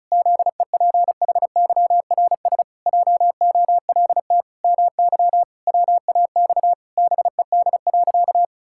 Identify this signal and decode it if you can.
Morse code, heard as a keyed tone, about 34 wpm, 700 Hz. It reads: ZEPHYRS JOLT MY WAX BED.
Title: Pangram Challenge v2 - 35wpm